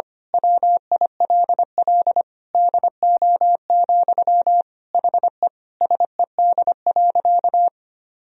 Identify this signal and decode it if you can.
Morse code, heard as a keyed tone, about 25 wpm, 700 Hz. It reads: WILL DO, HE SED.
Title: Sentences from Top 100 Words and Abbreviations 25wpm